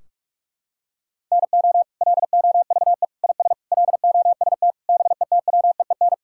Morse code, heard as a keyed tone, about 46 wpm, 700 Hz. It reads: NO LOVE IS LOST BETWEEN
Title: Sentences from Top 1600 Words 45wpm